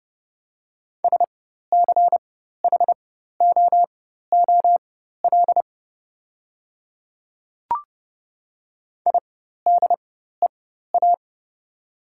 Morse code, heard as a keyed tone, about 30 wpm, spaced as for 14 wpm, 700 Hz. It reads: SCHOOL IDEA
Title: 4x Character Spacing - Top 300 Words - 30wpm